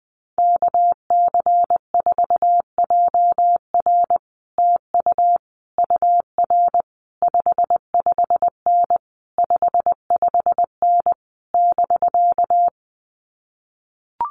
Morse code, heard as a keyed tone, about 20 wpm, 700 Hz. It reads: KC4JR TU UR 55N 55N <BK>
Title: Common QSO Phrases Encore 20wpm